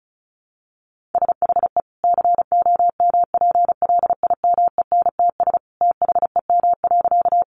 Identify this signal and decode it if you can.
Morse code, heard as a keyed tone, about 34 wpm, 700 Hz. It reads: SHE COMPLIMENTS THEM.
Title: Easy Phrases 35wpm